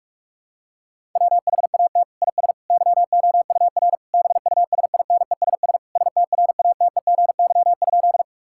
Morse code, heard as a keyed tone, about 47 wpm, 700 Hz. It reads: WHAT IS YOUR BUSINESS STRATEGY?
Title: Sentences from Top 1000 Words - Review Part 5 - 45wpm